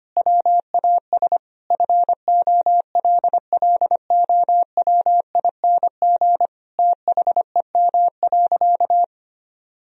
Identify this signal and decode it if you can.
Morse code, heard as a keyed tone, about 25 wpm, 700 Hz. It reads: WAS FOLLOWING THEM.
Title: Sentences from Top 1000 Words - Review Part 4 - 25wpm